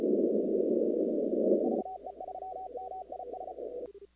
Morse code, heard as a keyed tone, about 34 wpm, 750 Hz, with noise. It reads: VE3MIS
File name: contests / 2019-cw-wpx-kt5j-k5pi- -2330 0-19